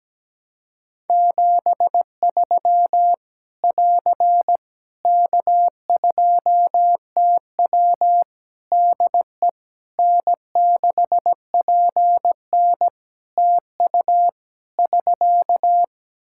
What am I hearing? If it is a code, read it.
Morse code, heard as a keyed tone, about 17 wpm, 700 Hz. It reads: 73 <AR> K2TW DE N6PN TU <SK>
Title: Practice QSO - Exchange 4 - Ending - 17wpm